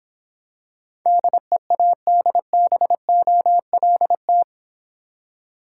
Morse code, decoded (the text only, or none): DEADBOLT